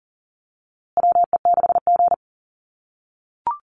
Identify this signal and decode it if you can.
Morse code, heard as a keyed tone, about 40 wpm, 700 Hz. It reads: WE6G
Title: USA Call Signs Repeated 3x 40wpm